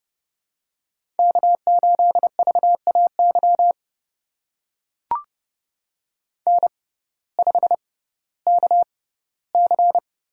Morse code, heard as a keyed tone, about 31 wpm, spaced as for 17 wpm, 700 Hz. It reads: K8VAY N5KC
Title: USA Call Signs - 5 Characters - 6x Character Spacing - 30wpm